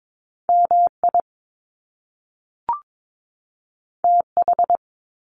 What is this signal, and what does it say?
Morse code, heard as a keyed tone, about 22 wpm, 700 Hz. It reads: MI TH
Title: Top Two Letter Combinations in Words 22wpm